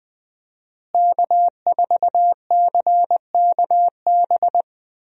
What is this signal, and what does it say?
Morse code, heard as a keyed tone, about 20 wpm, 700 Hz. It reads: K4CKB
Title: USA Call Signs - 5 Characters - Repeated 3x - 20wpm